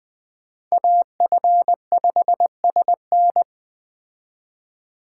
Morse code, heard as a keyed tone, about 21 wpm, 700 Hz. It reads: AF5SN
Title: Speed Racing - USA 2x2 Call Signs 17wpm